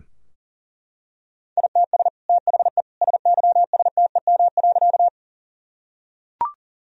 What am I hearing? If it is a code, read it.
Morse code, heard as a keyed tone, about 37 wpm, 700 Hz. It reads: ITS THE SYSTEM.
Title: Sentences from Top 1000 Words - Review Part 3 - 40wpm